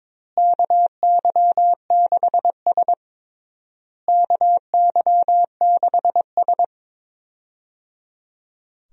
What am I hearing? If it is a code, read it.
Morse code, heard as a keyed tone, about 22 wpm, 700 Hz. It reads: KY6S KY6S
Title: USA Call Signs - 4 Characters - Repeated 3x - 22wpm